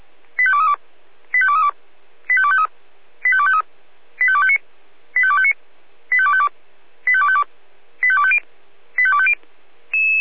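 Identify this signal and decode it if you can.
Five-tone selective call (ZVEI), heard as quick runs of five tones, tones of about 70 ms. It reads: 85232 85232 85253 85253 85259 85259 85252 85252 85250 85250
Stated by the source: ZVEI five-tone call received by SDR.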